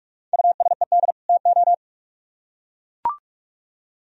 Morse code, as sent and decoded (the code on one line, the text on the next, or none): ..- ... -.. / - ---
USED TO